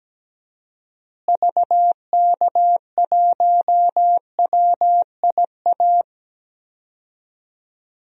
Morse code, heard as a keyed tone, about 17 wpm, 700 Hz. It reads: VK1WIA
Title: DXCC-100 Call Signs - 6 Characters - Repeated 3x - 17wpm